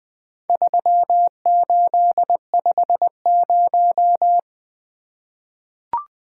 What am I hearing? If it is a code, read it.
Morse code, heard as a keyed tone, about 20 wpm, 700 Hz. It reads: 3850
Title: Four Letter-Number Combinations - 20wpm